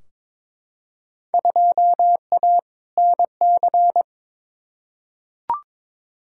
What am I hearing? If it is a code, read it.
Morse code, heard as a keyed tone, about 22 wpm, 700 Hz. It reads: 2A NC
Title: Field Day Exchanges - 22wpm